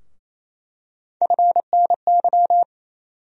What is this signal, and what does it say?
Morse code, heard as a keyed tone, about 28 wpm, 700 Hz. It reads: FNY
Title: USA Call Sign Postfixes - 3 Letters 28wpm